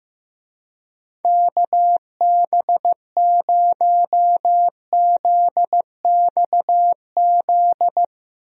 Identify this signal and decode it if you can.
Morse code, heard as a keyed tone, about 15 wpm, 700 Hz. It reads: KB0ZXZ
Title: USA Call Signs - 6 Characters - Repeated 3x - 15wpm